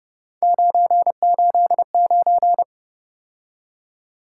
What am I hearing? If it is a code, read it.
Morse code, heard as a keyed tone, about 30 wpm, 700 Hz. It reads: 989
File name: Speed-Racing - Three Letter-Number Combinations - 25wpm